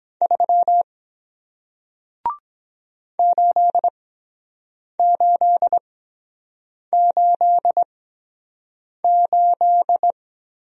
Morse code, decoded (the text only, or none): 3 8888